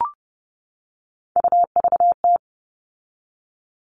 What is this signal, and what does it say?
Morse code, heard as a keyed tone, about 30 wpm, 700 Hz. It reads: UVT
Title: All Cut Numbers - 3-Digit 30wpm